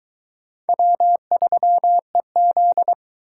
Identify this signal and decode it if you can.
Morse code, heard as a keyed tone, about 22 wpm, 700 Hz. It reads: W3EZ